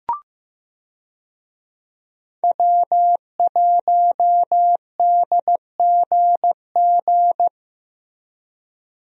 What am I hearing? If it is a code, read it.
Morse code, heard as a keyed tone, about 14 wpm, 700 Hz. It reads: W1DGG
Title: USA Call Signs - 5 Characters - Repeated 3x - 15wpm